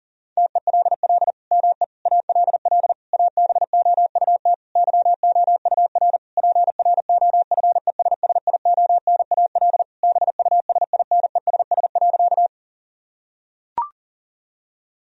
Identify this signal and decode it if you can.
Morse code, heard as a keyed tone, about 37 wpm, 700 Hz. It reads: TELL ME ALL ABOUT YOUR PROFESSIONAL BUSINESS.
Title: Sentences from Top 1000 Words - Review Part 4 - 40wpm